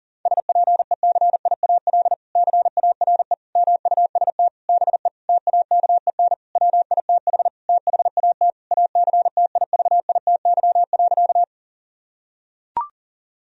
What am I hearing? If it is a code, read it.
Morse code, heard as a keyed tone, about 37 wpm, 700 Hz. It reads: SPECIAL CARE MUST BE TAKEN WITH THAT ACTIVITY.
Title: Sentences from Top 1000 Words - Review Part 4 - 40wpm